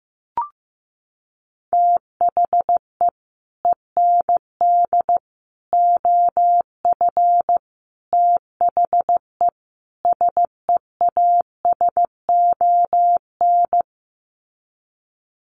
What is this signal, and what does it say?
Morse code, heard as a keyed tone, about 15 wpm, 700 Hz. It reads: THE END OF THE SEASON
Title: Sets of 5 Words Encore 15wpm